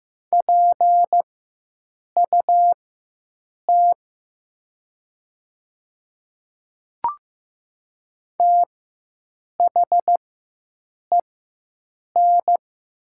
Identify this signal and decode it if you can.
Morse code, heard as a keyed tone, about 15 wpm, spaced as for 7 wpm, 700 Hz. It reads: PUT THEN
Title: 4x Character Spacing - Top 300 Words - 15wpm